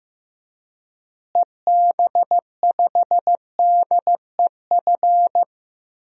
Morse code, heard as a keyed tone, about 15 wpm, 700 Hz. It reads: EB5DEF
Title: DXCC-100 Call Signs - 6 Characters - Repeated 3x - 15wpm